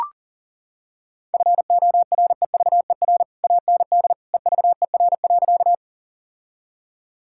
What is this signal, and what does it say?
Morse code, heard as a keyed tone, about 37 wpm, 700 Hz. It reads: FOREVER AND EVER.
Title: Sentences from Top 1600 Words 40wpm